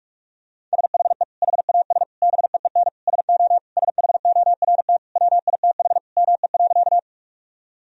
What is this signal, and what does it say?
Morse code, heard as a keyed tone, about 41 wpm, 700 Hz. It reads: SHE HAS BEEN SO SHORT WITH ME.
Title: Sentences from Top 1000 Words - Review Part 2 - 45wpm